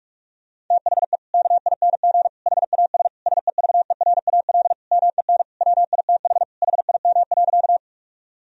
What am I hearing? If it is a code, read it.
Morse code, heard as a keyed tone, about 46 wpm, 700 Hz. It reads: THE KING HAS SEVERAL MEN WITH HIM.
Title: Sentences from Top 1000 Words - Review Part 2 - 45wpm